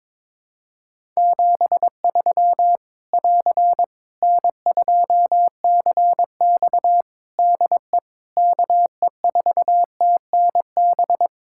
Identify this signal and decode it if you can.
Morse code, heard as a keyed tone, about 21 wpm, 700 Hz. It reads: 73 <AR> N2CX DE KE4TNB
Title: Practice QSO - Exchange 4 - Ending - 22wpm